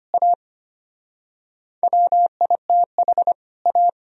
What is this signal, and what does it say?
Morse code, heard as a keyed tone, about 31 wpm, 700 Hz. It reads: A WITH A
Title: Speed-Racing - Sets of 2 Words 25wpm